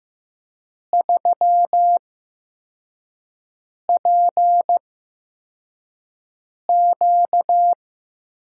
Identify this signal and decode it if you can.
Morse code, heard as a keyed tone, about 15 wpm, spaced as for 4 wpm, 700 Hz. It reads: 3PQ
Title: USA Call Signs - 5 Characters - 8x Character Spacing - 15wpm